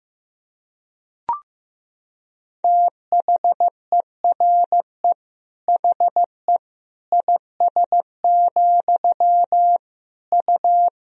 Morse code, heard as a keyed tone, about 15 wpm, 700 Hz. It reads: THERE HE IS, U
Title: Sentences from Top 100 Words and Abbreviations 15wpm